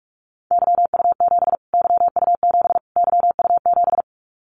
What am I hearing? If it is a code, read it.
Morse code, heard as a keyed tone, about 44 wpm, 700 Hz. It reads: YU7 YU7 YU7
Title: Callsign Prefixes for Top 25 DXCC - Repeated 3x - 45wpm